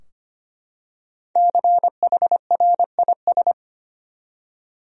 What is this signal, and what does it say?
Morse code, heard as a keyed tone, about 25 wpm, 700 Hz. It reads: CHRIS
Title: USA Common Names - 25wpm